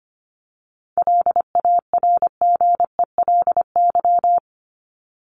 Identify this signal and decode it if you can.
Morse code, heard as a keyed tone, about 25 wpm, 700 Hz. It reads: LARGELY
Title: LY Suffix - Top 100 Words with Base Words 25wpm